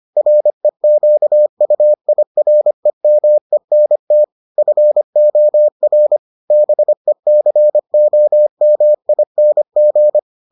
Morse code, heard as a keyed tone, about 25 wpm, 600 Hz. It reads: REQUIREMENT FOR BECOMING